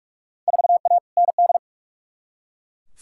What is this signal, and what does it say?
Morse code, heard as a keyed tone, about 47 wpm, 700 Hz. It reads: 4A ND
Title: Field Day Exchanges Encore - 45wpm